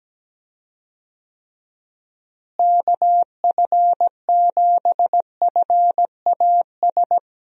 Morse code, heard as a keyed tone, about 17 wpm, 700 Hz. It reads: KF7FAS
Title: USA Call Signs - 6 Characters - Repeated 2x - 17wpm